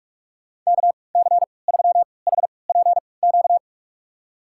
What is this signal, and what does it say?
Morse code, heard as a keyed tone, about 46 wpm, spaced as for 26 wpm, 700 Hz. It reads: KC3HPQ
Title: USA Call Signs - 6 Characters - 3x Character Spacing - 45wpm